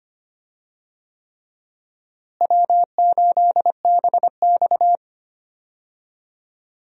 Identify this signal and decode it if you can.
Morse code, heard as a keyed tone, about 25 wpm, 700 Hz. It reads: W8BX
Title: USA Call Signs - 4 Characters - Repeated 3x - 25wpm